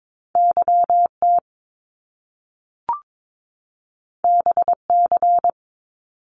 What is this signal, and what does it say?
Morse code, heard as a keyed tone, about 22 wpm, 700 Hz. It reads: YT BC
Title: Canadian Provinces and Territories 22wpm